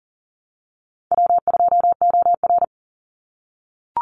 Morse code, heard as a keyed tone, about 40 wpm, 700 Hz. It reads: W2OR
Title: Easy Call Signs - 40wpm